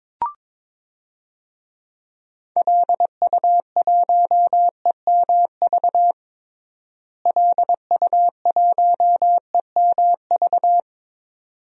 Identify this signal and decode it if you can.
Morse code, heard as a keyed tone, about 21 wpm, 700 Hz. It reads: LU1EMV LU1EMV
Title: DXCC-100 Call Signs - 6 Characters - Repeated 2x - 22wpm